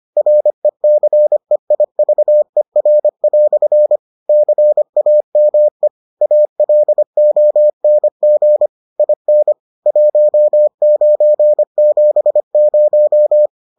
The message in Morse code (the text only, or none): RECEIVER" CAME ALONG IN 1970